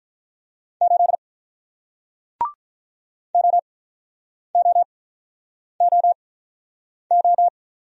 Morse code, heard as a keyed tone, about 48 wpm, spaced as for 9 wpm, 700 Hz. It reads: 8 OOOO